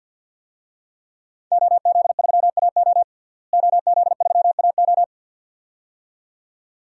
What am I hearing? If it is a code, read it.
Morse code, heard as a keyed tone, about 48 wpm, 700 Hz. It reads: OZ3AO OZ3AO
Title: DXCC-100 Call Signs - 5 Characters - Repeated 2x - 50wpm